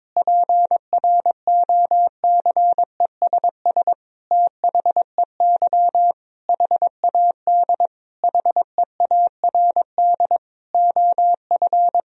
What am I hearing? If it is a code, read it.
Morse code, heard as a keyed tone, about 21 wpm, 700 Hz. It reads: PROCESS THEY HAD HEARD OF
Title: Sentences from Top 1000 Words - Review Part 3 - 22wpm